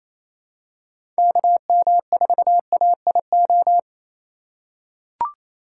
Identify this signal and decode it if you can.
Morse code, heard as a keyed tone, about 27 wpm, 700 Hz. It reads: KM4AIO
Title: USA Call Signs - 6 Characters - 4x Character Spacing - 28wpm